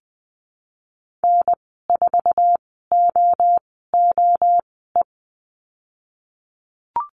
Morse code, heard as a keyed tone, about 20 wpm, spaced as for 15 wpm, 700 Hz. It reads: N4OOE
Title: 062 Lesson 11g - Letter h - Call Signs 20wpm